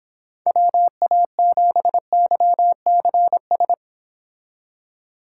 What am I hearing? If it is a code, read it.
Morse code, heard as a keyed tone, about 26 wpm, 700 Hz. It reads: WA7YCS